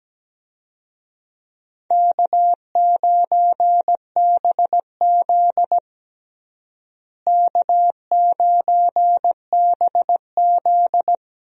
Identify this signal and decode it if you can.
Morse code, heard as a keyed tone, about 17 wpm, 700 Hz. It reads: K9BZ K9BZ
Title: USA Call Signs - 4 Characters - Repeated 3x - 17wpm